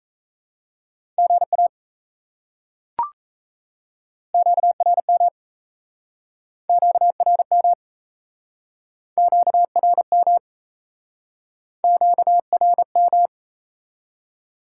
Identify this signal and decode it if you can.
Morse code, heard as a keyed tone, about 37 wpm, 700 Hz. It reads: GA QRM QRM QRM QRM